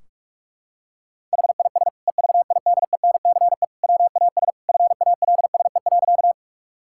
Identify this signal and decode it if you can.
Morse code, heard as a keyed tone, about 46 wpm, 700 Hz. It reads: HIS EVIDENCE WAS FALSE.
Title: Sentences from Top 1600 Words 45wpm